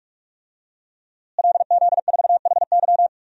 Morse code, heard as a keyed tone, about 46 wpm, 700 Hz. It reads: LZ4HY
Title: DXCC-100 Call Signs - 5 Characters - Repeated 3x - 45wpm